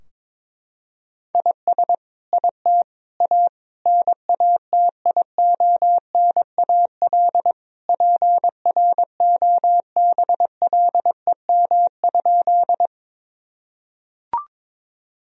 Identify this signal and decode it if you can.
Morse code, heard as a keyed tone, about 22 wpm, 700 Hz. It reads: IS IT A NATIONAL PROBLEM?
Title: Sentences from Top 1000 Words - Review Part 4 - 22wpm